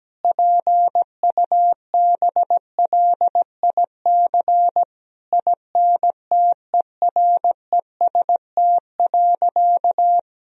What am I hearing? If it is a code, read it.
Morse code, heard as a keyed tone, about 17 wpm, 700 Hz. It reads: PUBLIC INTEREST.
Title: Sentences from Top 1000 Words - Review Part 4 - 17wpm